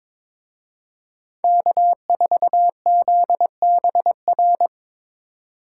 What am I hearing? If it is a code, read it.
Morse code, heard as a keyed tone, about 25 wpm, 700 Hz. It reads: K4ZBR
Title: Speed Racing - USA 1x3 Call Signs 22wpm